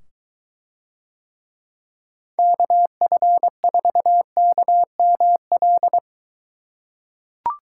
Morse code, heard as a keyed tone, about 22 wpm, 700 Hz. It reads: KF4KML